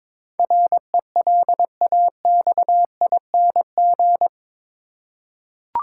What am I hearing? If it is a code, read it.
Morse code, heard as a keyed tone, about 22 wpm, 700 Hz. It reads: RELAXING
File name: practice / 22wpm / Eight Letter Words - 22wpm